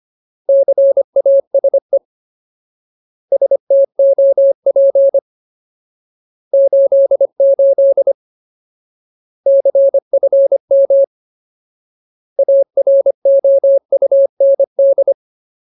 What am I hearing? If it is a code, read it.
Morse code, heard as a keyed tone, about 25 wpm, 550 Hz. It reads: CASE STOP 88 CFM AROUND
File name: Ultimate Word List / 25WPM Ultimate